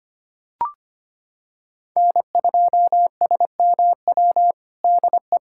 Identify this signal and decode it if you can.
Morse code, heard as a keyed tone, about 25 wpm, 700 Hz. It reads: N2SMW DE
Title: USA Callsign DE Callsign - 5 Characters - 25wpm